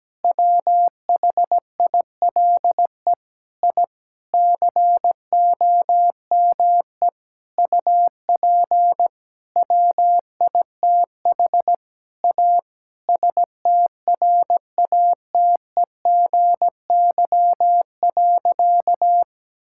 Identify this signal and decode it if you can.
Morse code, heard as a keyed tone, about 17 wpm, 700 Hz. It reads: WHILE I COME UP WITH A STRATEGY.
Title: Sentences from Top 1000 Words - Review Part 5 - 17wpm